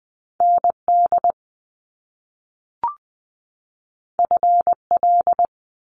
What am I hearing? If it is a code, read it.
Morse code, heard as a keyed tone, about 20 wpm, 700 Hz. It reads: ND FL
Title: US States 20wpm